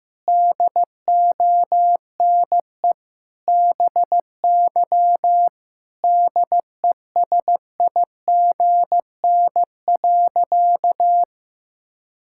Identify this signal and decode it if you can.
Morse code, heard as a keyed tone, about 16 wpm, 700 Hz. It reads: DONE BY DESIGN.
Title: Sentences from Top 1000 Words - Review Part 5 - 15wpm